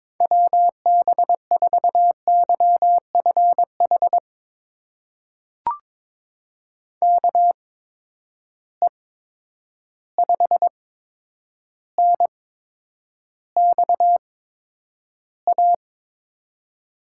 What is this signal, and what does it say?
Morse code, heard as a keyed tone, about 22 wpm, spaced as for 9 wpm, 700 Hz. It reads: WB4YFH KE5NXA
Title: USA Call Signs - 6 Characters - 8x Character Spacing - 22wpm